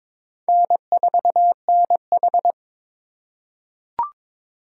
Morse code, heard as a keyed tone, about 22 wpm, 700 Hz. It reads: N4NH
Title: USA Call Signs - 4 Characters - 2x Character Spacing - 22wpm